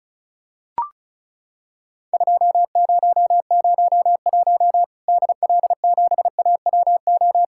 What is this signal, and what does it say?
Morse code, heard as a keyed tone, about 36 wpm, 700 Hz. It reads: 2001 DL7AWO
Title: Worked All Europe DX Contest - QTC Exchange - 35wpm